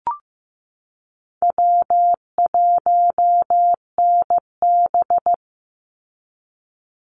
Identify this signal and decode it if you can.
Morse code, heard as a keyed tone, about 15 wpm, 700 Hz. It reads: W1NB
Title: USA Call Signs 15wpm